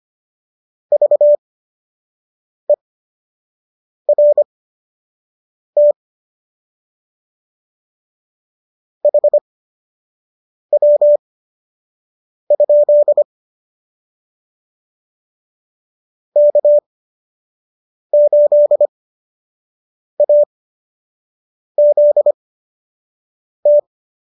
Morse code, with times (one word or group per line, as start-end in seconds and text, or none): 0.9-5.9 VERT
9.1-13.2 HW?
16.4-23.8 K8AZT